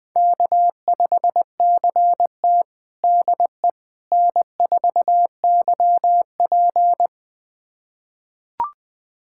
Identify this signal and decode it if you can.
Morse code, heard as a keyed tone, about 20 wpm, 700 Hz. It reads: K5CT DE N4YP
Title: USA Callsign DE Callsign - 4 Characters - 20wpm